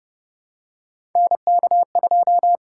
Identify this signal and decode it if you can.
Morse code, heard as a keyed tone, about 30 wpm, 690 Hz. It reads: NK2
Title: USA Call Signs - 4 Characters - 6x Character Spacing - 30wpm